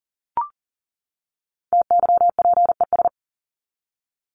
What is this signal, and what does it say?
Morse code, heard as a keyed tone, about 40 wpm, 700 Hz. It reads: TYPES
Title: ES:S Suffix - Top 100 Words with Base Words 40wpm